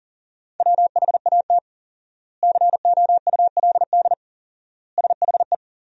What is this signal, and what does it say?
Morse code, heard as a keyed tone, about 40 wpm, 700 Hz. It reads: WHAT COULD SHE
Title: 4x Word-Spacing - Sentences from Top 200 Words - 40wpm